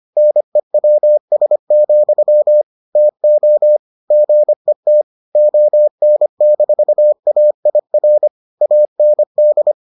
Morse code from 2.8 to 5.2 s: TO GET

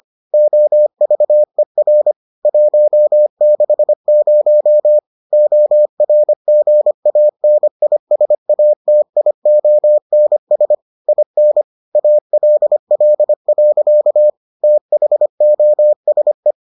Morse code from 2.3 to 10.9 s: 160 ORGANISATIONS

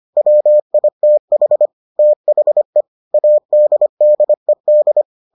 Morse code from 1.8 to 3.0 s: THE

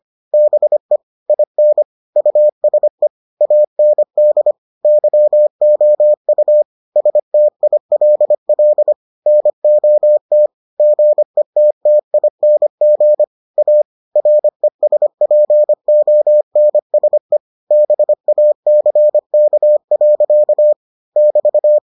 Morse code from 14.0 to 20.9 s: RESPONSE BACK.

